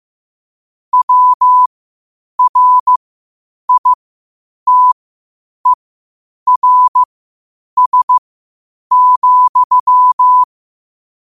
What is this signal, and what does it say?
Morse code, heard as a keyed tone, about 15 wpm, spaced as for 8 wpm, 1000 Hz. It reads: WRITERS,
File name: qotdfiles / 8wpm